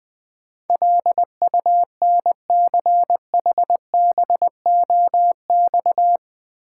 Morse code, heard as a keyed tone, about 20 wpm, 700 Hz. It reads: LUNCHBOX
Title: Easy Words with Hints - 20wpm